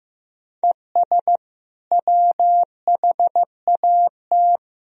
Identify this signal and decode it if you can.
Morse code, heard as a keyed tone, about 15 wpm, 700 Hz. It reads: ES WHAT
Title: Sentences from Top 100 Words and Abbreviations 15wpm